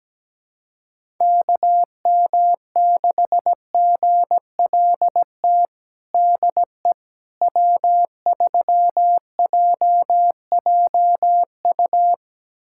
Morse code, heard as a keyed tone, about 17 wpm, 700 Hz. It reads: KM6GLT DE W3JJU
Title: USA Callsign DE Callsign 17wpm